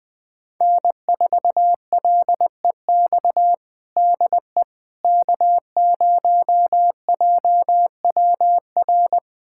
Morse code, heard as a keyed tone, about 21 wpm, 700 Hz. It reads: N4LEX DE K0JWR
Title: USA Callsign DE Callsign - 5 Characters - 20wpm